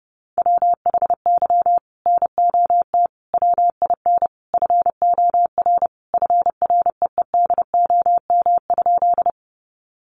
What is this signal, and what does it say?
Morse code, heard as a keyed tone, about 30 wpm, 700 Hz. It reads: WHY NOT WIN FOR FREEDOM?
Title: Sentences from Top 1500 Words 30wpm